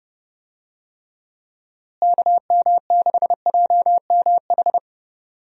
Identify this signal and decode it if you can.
Morse code, heard as a keyed tone, about 31 wpm, 700 Hz. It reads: KM6JMH
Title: USA Call Signs - 6 Characters - Repeated 3x - 30wpm